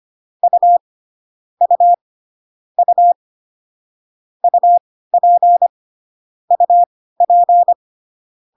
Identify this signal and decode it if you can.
Morse code, heard as a keyed tone, about 25 wpm, spaced as for 16 wpm, 700 Hz. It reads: U U U UP UP